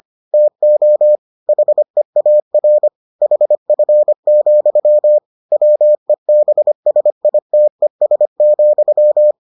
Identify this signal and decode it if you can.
Morse code, heard as a keyed tone, about 25 wpm, 600 Hz. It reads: TO HEAR HF, WEBSITES,